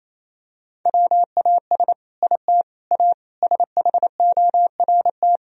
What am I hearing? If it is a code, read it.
Morse code, heard as a keyed tone, about 28 wpm, 700 Hz. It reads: WAS IT A SHORT
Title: Sentences from Top 1000 Words - Review Part 4 - 28wpm